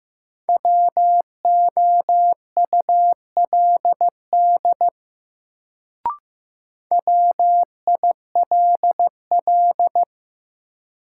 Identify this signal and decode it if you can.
Morse code, heard as a keyed tone, about 15 wpm, 700 Hz. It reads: WOULD WILL
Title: Sentences from Top 100 Words - New Word Drill - 15wpm